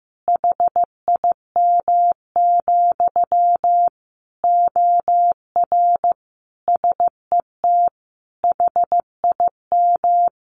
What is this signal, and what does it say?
Morse code, heard as a keyed tone, about 15 wpm, 700 Hz. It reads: HIM, OR SET HIM
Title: Sentences from Top 1100 Words 15wpm